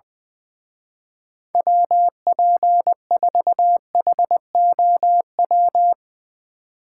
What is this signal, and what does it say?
Morse code, heard as a keyed tone, about 20 wpm, 700 Hz. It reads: WP4HOW
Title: Speed Racing - USA 2x3 Call Signs 20wpm